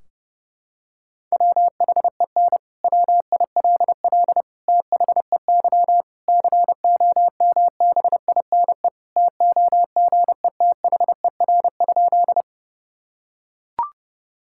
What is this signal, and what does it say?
Morse code, heard as a keyed tone, about 30 wpm, 700 Hz. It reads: WHEN WILL THEY COMBINE TOGETHER?
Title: Sentences from Top 1600 Words 30wpm